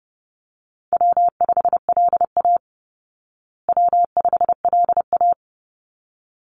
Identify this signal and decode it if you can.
Morse code, heard as a keyed tone, about 28 wpm, 700 Hz. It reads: W5LA W5LA
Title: Call Signs Repeated 3x - 30wpm